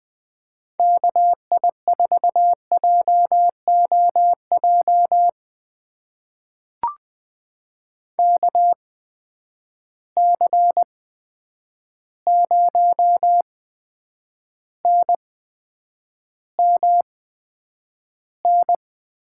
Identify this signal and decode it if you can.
Morse code, heard as a keyed tone, about 20 wpm, spaced as for 9 wpm, 700 Hz. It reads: KI4JOJ KC0NMN
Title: USA Call Signs - 6 Characters - 8x Character Spacing - 20wpm